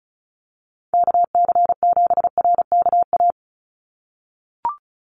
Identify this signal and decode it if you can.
Morse code, heard as a keyed tone, about 34 wpm, 700 Hz. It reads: KC7RKA
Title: USA Call Signs 35wpm